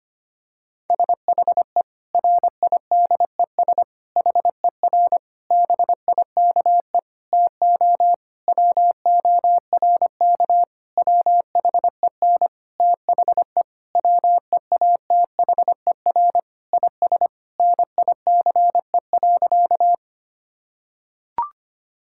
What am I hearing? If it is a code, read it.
Morse code, heard as a keyed tone, about 25 wpm, 700 Hz. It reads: SHE RIDES HER BIKE TO WORK WHEN THE WEATHER IS NICE.
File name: practice / 25wpm / Difficult Sentences with Hints - 25wpm